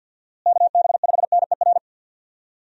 Morse code, heard as a keyed tone, about 53 wpm, 700 Hz. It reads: KB5NER
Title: USA Call Signs - 6 Characters - Repeated 2x - 50wpm